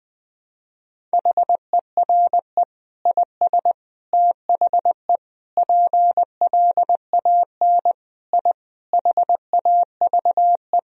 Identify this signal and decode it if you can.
Morse code, heard as a keyed tone, about 20 wpm, 700 Hz. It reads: HERE IS THE PLAN I HAVE
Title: Sentences from Top 1000 Words - Review Part 2 - 20wpm